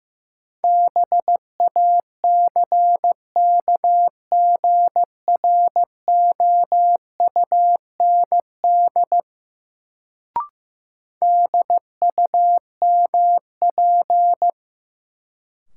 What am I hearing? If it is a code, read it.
Morse code, heard as a keyed tone, about 15 wpm, 700 Hz. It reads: BACKGROUND DUMP
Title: Sentences from Top 1300 Words - New Word Drill - 15wpm